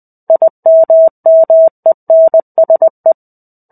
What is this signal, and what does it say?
Morse code, heard as a keyed tone, about 20 wpm, 650 Hz. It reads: IMMENSE
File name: 05-Chapter-05 20 WPM